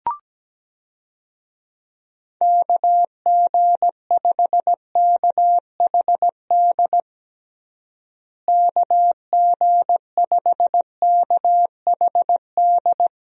Morse code, heard as a keyed tone, about 17 wpm, 700 Hz. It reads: KG5KHD KG5KHD
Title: USA Call Signs - 6 Characters - Repeated 3x - 17wpm